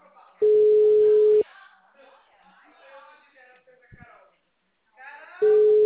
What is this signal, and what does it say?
Ringback tone (Europe, 425 Hz), with speech.